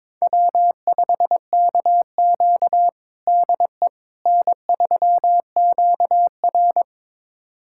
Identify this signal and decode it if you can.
Morse code, heard as a keyed tone, about 22 wpm, 700 Hz. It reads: W5KQ DE N3QR
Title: USA Callsign DE Callsign - 4 Characters - 22wpm